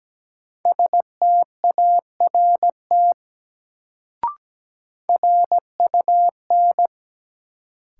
Morse code, heard as a keyed tone, about 17 wpm, 700 Hz. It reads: START RUN
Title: Sentences from Top 300 Words - New Word Drill - 17wpm